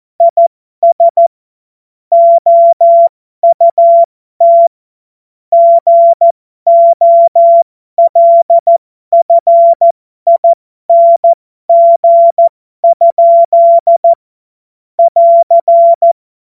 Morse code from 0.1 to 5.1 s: IS OUT